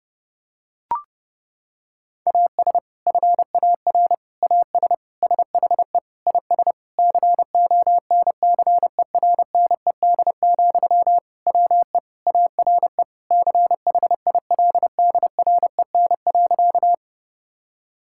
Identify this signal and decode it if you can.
Morse code, heard as a keyed tone, about 30 wpm, 700 Hz. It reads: AS FAR AS SHE IS CONCERNED, WE ARE CHILDREN.
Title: Sentences from Top 1000 Words - Review Part 6 - 30wpm